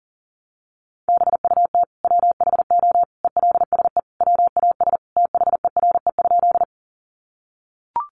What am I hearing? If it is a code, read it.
Morse code, heard as a keyed tone, about 40 wpm, 700 Hz. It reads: BUT WHO ELSE WAS THERE?
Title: Sentences from Top 800 Words 40wpm